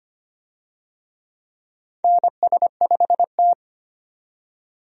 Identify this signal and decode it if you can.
Morse code, heard as a keyed tone, about 25 wpm, 700 Hz. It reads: NS5T
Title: USA Call Signs - 4 Characters - 25wpm